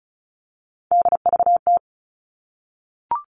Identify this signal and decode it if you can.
Morse code, heard as a keyed tone, about 34 wpm, 700 Hz. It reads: DVT
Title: All Cut Numbers - 3-Digit 35wpm